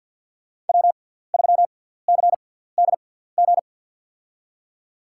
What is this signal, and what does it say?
Morse code, heard as a keyed tone, about 52 wpm, spaced as for 17 wpm, 700 Hz. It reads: W3CDG
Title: USA Call Signs - 5 Characters - 6x Character Spacing - 50wpm